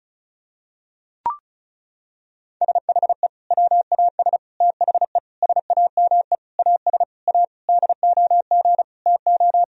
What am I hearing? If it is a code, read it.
Morse code, heard as a keyed tone, about 36 wpm, 700 Hz. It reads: SHE WAS THE SAME AS A DOG TO
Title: Sentences from Top 1000 Words - Review Part 2 - 35wpm